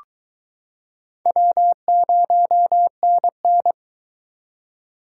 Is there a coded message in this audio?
Yes: Morse code from 1.3 to 3.7 s, heard as a keyed tone, about 22 wpm, 700 Hz. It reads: W0NN